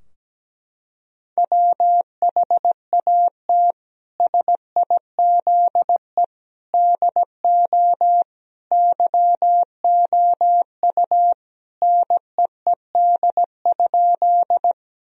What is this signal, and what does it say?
Morse code, heard as a keyed tone, about 17 wpm, 700 Hz. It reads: WHAT SIZE DO YOU NEED?
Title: Sentences from Top 1000 Words - Review Part 2 - 17wpm